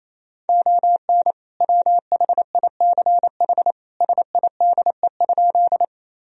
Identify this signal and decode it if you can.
Morse code, heard as a keyed tone, about 28 wpm, 700 Hz. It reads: ON WHICH SIDE?
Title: Sentences from Top 1000 Words - Review Part 1 - 28wpm